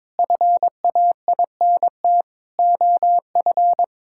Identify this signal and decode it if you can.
Morse code, heard as a keyed tone, about 22 wpm, 700 Hz. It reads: FAINT OF
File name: Sentences with 7-Word Phrases - 22wpm